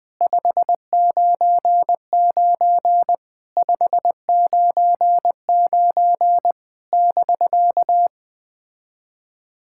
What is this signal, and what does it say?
Morse code, heard as a keyed tone, about 20 wpm, 700 Hz. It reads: 599 599 <BK>
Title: POTA QSO Practice - 20wpm